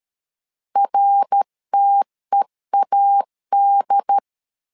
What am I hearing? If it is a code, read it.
Morse code, heard as a keyed tone, about 13 wpm, 800 Hz. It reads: RTEAD
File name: LEZIONI / 3 MWRO / 3 MWR 12wpm 1711